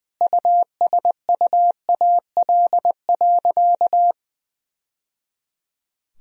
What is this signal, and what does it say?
Morse code, heard as a keyed tone, about 20 wpm, 700 Hz. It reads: USUAL.
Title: Sentences from Top 1000 Words - Review Part 2 - 20wpm